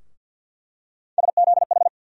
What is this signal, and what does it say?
Morse code, heard as a keyed tone, about 52 wpm, 700 Hz. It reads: SZH
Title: USA Call Sign Postfixes - 3 Letters 50wpm